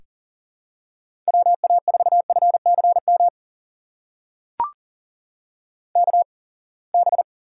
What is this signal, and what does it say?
Morse code, heard as a keyed tone, about 40 wpm, spaced as for 28 wpm, 700 Hz. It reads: WA4FCM KB
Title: USA Call Signs - 6 Characters - 8x Character Spacing - 40wpm